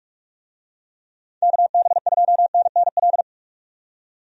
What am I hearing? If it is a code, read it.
Morse code, heard as a keyed tone, about 44 wpm, 700 Hz. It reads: KB2NNL
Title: USA Call Signs - 6 Characters - 45wpm